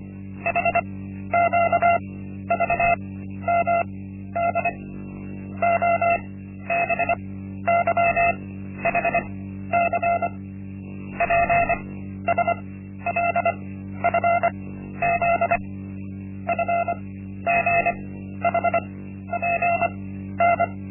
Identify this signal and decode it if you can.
Morse code, heard as a keyed tone, about 25 wpm, spaced as for 12 wpm, 690 Hz. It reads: HQVMD OBYHC PSLFZ FGHPN
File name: Linksmakalnis-1